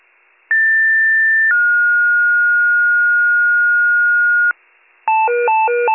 Two-tone page, heard as a one-second tone, then a three-second tone, about 1800 then 1430 Hz, with noise. It is heard from 0.5 to 4.5 s.